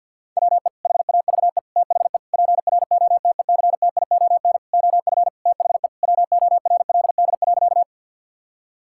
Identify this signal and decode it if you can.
Morse code, heard as a keyed tone, about 50 wpm, 700 Hz. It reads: WE HAVE THE PROTECTION OF THE WORLD.
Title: Sentences from Top 1600 Words 50wpm